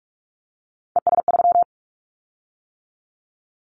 Morse code, heard as a keyed tone, about 44 wpm, 700 Hz. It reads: ES3
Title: Callsign Prefixes for Top 50 DXCC - 45wpm